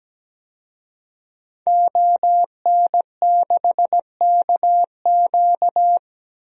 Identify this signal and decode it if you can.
Morse code, heard as a keyed tone, about 17 wpm, 700 Hz. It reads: ON6KQ
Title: DXCC-100 Call Signs - 5 Characters - 17wpm